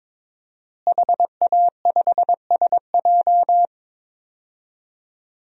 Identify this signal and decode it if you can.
Morse code, heard as a keyed tone, about 22 wpm, 700 Hz. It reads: HA5SJ
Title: DXCC-100 Call Signs - 5 Characters - Repeated 3x - 22wpm